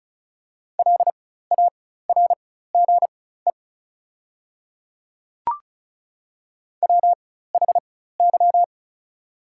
Morse code, heard as a keyed tone, about 34 wpm, spaced as for 16 wpm, 690 Hz. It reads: LARGE WHY
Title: 4x Character Spacing - Top 300 Words - 35wpm